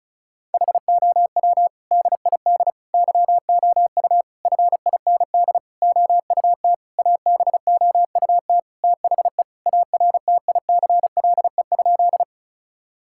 Morse code, heard as a keyed tone, about 36 wpm, 700 Hz. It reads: HOW DID YOU FIND OUT ABOUT THE ARTICLE?
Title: Sentences from Top 1000 Words - Review Part 5 - 35wpm